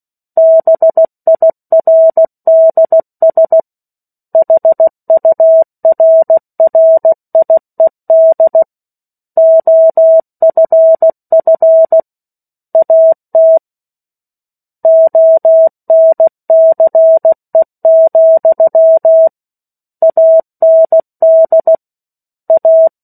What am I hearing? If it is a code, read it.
Morse code, heard as a keyed tone, about 16 wpm, 650 Hz. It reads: BIRDS HURRIED OFF AT ONCE, AND A
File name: 03-Chapter-03 16 WPM